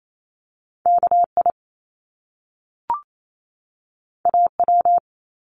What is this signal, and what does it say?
Morse code, heard as a keyed tone, about 28 wpm, 700 Hz. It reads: KI AW
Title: Top Two Letter Combinations in Words 28wpm